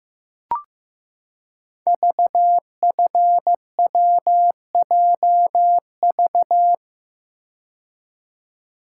Morse code, heard as a keyed tone, about 15 wpm, 700 Hz. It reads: VFWJV
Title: Code Groups - 5 Letters - 15wpm